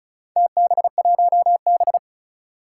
Morse code, heard as a keyed tone, about 36 wpm, 700 Hz. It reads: TB1B
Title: Four Challenging Letter-Number Combinations 35wpm